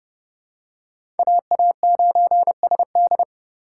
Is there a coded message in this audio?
Yes: Morse code from 1.2 to 3.2 s, heard as a keyed tone, about 28 wpm, 700 Hz. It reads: AA9SD